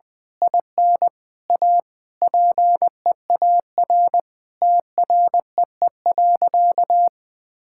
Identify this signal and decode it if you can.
Morse code, heard as a keyed tone, about 20 wpm, 700 Hz. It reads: IN A PEAR TREE.
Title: Sentences with 9-Word Phrases - Spoken Beginning - 20wpm